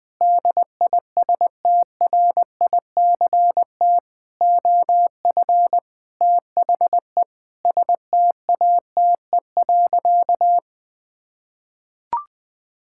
Morse code, heard as a keyed tone, about 20 wpm, 700 Hz. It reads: DISTRICT OF THE STATE.
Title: Sentences from Top 1600 Words 20wpm